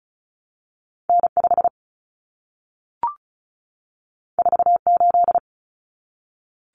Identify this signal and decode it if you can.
Morse code, heard as a keyed tone, about 36 wpm, 700 Hz. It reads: N5 48
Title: Common Cut Numbers - 2-Digit 35wpm